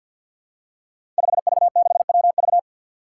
Morse code, heard as a keyed tone, about 52 wpm, 700 Hz. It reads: 5V6WV
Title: Five Challenging Letter-Number Combinations 50wpm